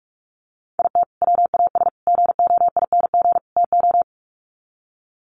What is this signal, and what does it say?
Morse code, heard as a keyed tone, about 44 wpm, 700 Hz. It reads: IT WAS GOING TO
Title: Sets of 4 Words 45wpm